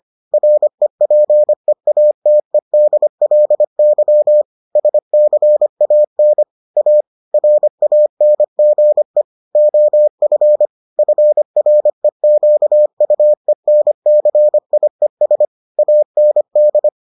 Morse code, heard as a keyed tone, about 25 wpm, 600 Hz. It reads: REPEATEDLY SCAN A RANGE OF FREQUENCIES AND